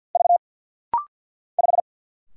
Morse code, heard as a keyed tone, about 48 wpm, 700 Hz. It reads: V 5
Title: Single Letter-Number - Rapid-Fire 50wpm